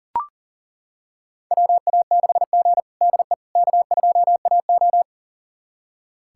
Morse code, heard as a keyed tone, about 40 wpm, 700 Hz. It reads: WA6G DE K2AO
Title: USA Callsign DE Callsign - 4 Characters - 40wpm